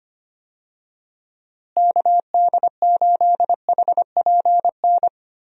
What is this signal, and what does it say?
Morse code, heard as a keyed tone, about 25 wpm, 700 Hz. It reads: KD8HPN
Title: USA Call Signs - 6 Characters - Repeated 3x - 25wpm